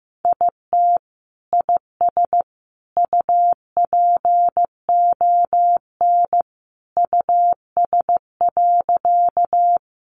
Morse code, heard as a keyed tone, about 14 wpm, 700 Hz. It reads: IT IS UPON US.
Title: Sentences from Top 1100 Words 15wpm